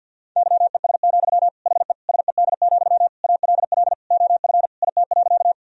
Morse code, heard as a keyed tone, about 52 wpm, 700 Hz. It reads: YES, HE SED, ALL OF IT.
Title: Sentences from Top 500 Words and Abbreviations 50wpm